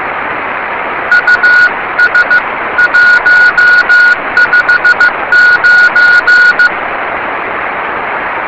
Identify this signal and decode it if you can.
Morse code, heard as a keyed tone, about 15 wpm, 1450 Hz. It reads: US 159
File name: yubileiry cw by dk3wn